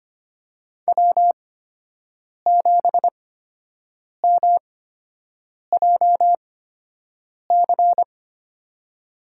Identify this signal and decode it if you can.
Morse code, heard as a keyed tone, about 25 wpm, spaced as for 7 wpm, 700 Hz. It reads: W7MJC
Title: USA Call Signs - 8x Character Spacing - 25wpm